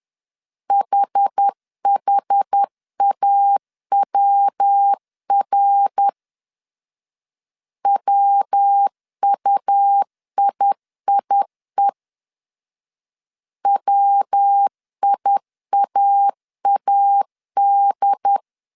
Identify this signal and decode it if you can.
Morse code, heard as a keyed tone, about 10 wpm, 800 Hz. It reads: HHAWR WUIIE WIAAD
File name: CW / DET / 1 3 1500C S B 10wpm 3335